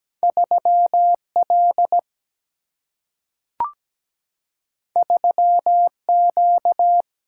Morse code, heard as a keyed tone, about 17 wpm, 700 Hz. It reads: 3 L 3 Q
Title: Single-Letter Single-Number 17wpm